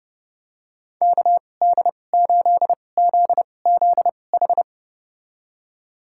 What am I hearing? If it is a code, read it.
Morse code, heard as a keyed tone, about 32 wpm, spaced as for 22 wpm, 700 Hz. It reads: KD8ZZH
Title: USA Call Signs - 6 Characters - 2x Character Spacing - 30wpm